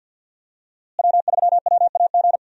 Morse code, heard as a keyed tone, about 46 wpm, 700 Hz. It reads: W3WAG
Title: USA Call Signs - 5 Characters - Repeated 2x - 50wpm